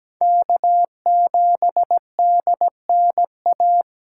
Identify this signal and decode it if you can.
Morse code, heard as a keyed tone, about 17 wpm, 700 Hz. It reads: K7DNA
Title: USA Call Signs - 5 Characters - 3x Character Spacing - 17wpm